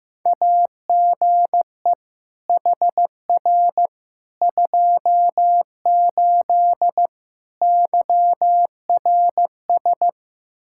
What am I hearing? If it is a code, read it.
Morse code, heard as a keyed tone, about 16 wpm, 700 Hz. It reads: AGE HR 28 YRS
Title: Practice QSO - Exchange 3 - More Info - 15wpm